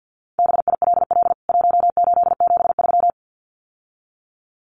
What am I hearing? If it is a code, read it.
Morse code, heard as a keyed tone, about 50 wpm, 700 Hz. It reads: BILL 1873
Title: CWT Exchange 50wpm